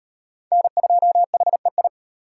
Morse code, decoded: N2HEI

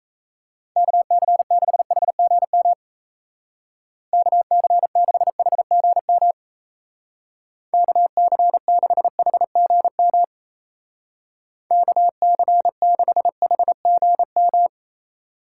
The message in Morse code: KC6HGM KC6HGM KC6HGM KC6HGM

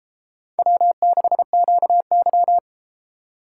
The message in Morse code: W6QY

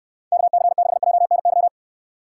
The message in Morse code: KC6JNY

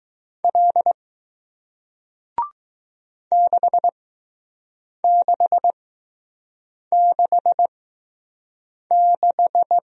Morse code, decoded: L 6666

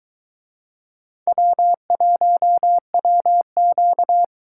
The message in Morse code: W1WQ

